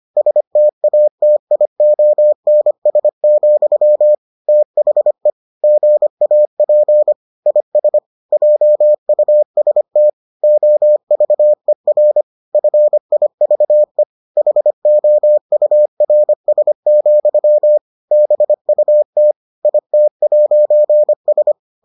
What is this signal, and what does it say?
Morse code, heard as a keyed tone, about 25 wpm, 600 Hz. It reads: STATIONS, THE GAP IS JUST OVER FIVE HOURS, BUT IT'S